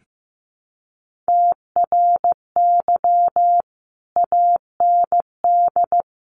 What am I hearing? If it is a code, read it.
Morse code, heard as a keyed tone, about 15 wpm, 700 Hz. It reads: TRY AND
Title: Sentences from Top 1200 Words 15wpm